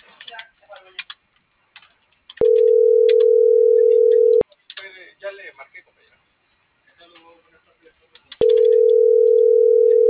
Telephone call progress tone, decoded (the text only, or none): ringback tone